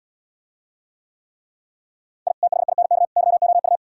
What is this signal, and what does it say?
Morse code, heard as a keyed tone, about 79 wpm, 700 Hz. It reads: I THINK YOU